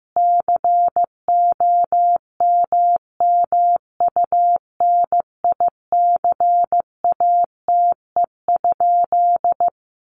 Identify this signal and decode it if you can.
Morse code, heard as a keyed tone, about 15 wpm, 700 Hz. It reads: COMMUNICATE?
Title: Sentences from Top 1300 Words 15wpm